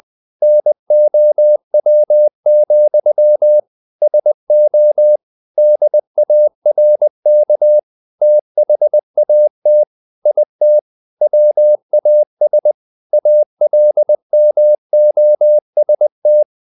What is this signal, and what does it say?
Morse code, heard as a keyed tone, about 20 wpm, 600 Hz. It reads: NOW, SO DARK THAT IT WAS ALMOST
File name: the old man and the sea / the old man and the sea part 1 20wpm